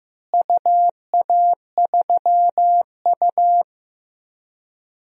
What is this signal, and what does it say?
Morse code, heard as a keyed tone, about 15 wpm, 700 Hz. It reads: UA3U
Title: DXCC-100 Call Signs - 4 Characters - Repeated 2x - 15wpm